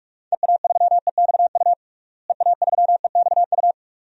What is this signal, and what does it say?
Morse code, heard as a keyed tone, about 46 wpm, 700 Hz. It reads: EA3EXU EA3EXU
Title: DXCC-100 Call Signs - 6 Characters - Repeated 2x - 45wpm